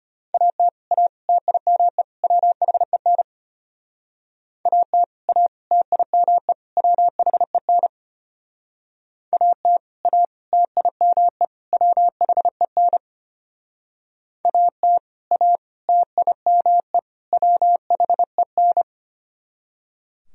Morse code, heard as a keyed tone, about 30 wpm, 700 Hz. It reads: AT A TIME WHEN AT A TIME WHEN AT A TIME WHEN AT A TIME WHEN
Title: Speed-Racing - Sets of 4 Words 25wpm